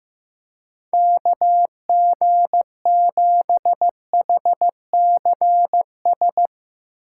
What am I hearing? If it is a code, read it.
Morse code, heard as a keyed tone, about 15 wpm, 700 Hz. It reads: KG7HCS
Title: Speed Racing - USA 2x3 Call Signs 15wpm